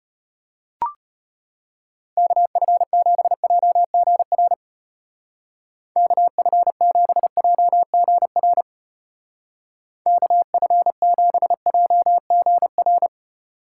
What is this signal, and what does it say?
Morse code, heard as a keyed tone, about 31 wpm, 700 Hz. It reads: KF7JGR KF7JGR KF7JGR
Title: Speed Racing - USA 2x3 Call Signs 25wpm